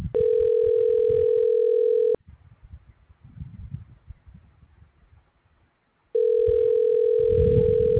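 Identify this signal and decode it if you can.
Ringback tone (North America), 440 + 480 Hz.
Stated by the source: Outbound call, not answered, forwarded to an automatic voicemail system.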